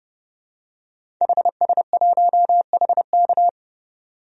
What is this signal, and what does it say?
Morse code, heard as a keyed tone, about 30 wpm, 700 Hz. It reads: HS1HK
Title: DXCC-100 Call Signs - 5 Characters - 30wpm